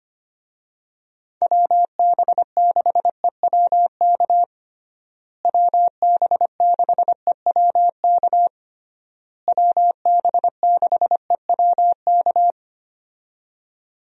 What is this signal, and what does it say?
Morse code, heard as a keyed tone, about 25 wpm, 700 Hz. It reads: WB6EWK WB6EWK WB6EWK
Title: USA Call Signs - 6 Characters - Repeated 3x - 25wpm